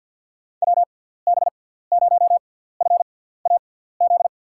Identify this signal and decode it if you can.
Morse code, heard as a keyed tone, about 54 wpm, spaced as for 17 wpm, 700 Hz. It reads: WB0FAZ
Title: USA Call Signs - 6 Characters - 6x Character Spacing - 50wpm